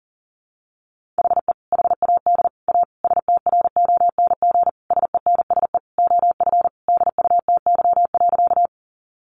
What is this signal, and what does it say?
Morse code, heard as a keyed tone, about 40 wpm, 700 Hz. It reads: HE HAD A STRONG SENSE OF DUTY.
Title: Sentences from Top 1500 Words 40wpm